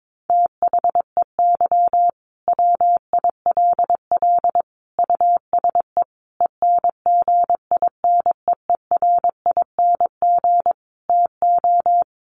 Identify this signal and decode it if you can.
Morse code, heard as a keyed tone, about 22 wpm, 700 Hz. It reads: THEY WILL USE ENGINEERING TO
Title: Sentences from Top 1500 Words 22wpm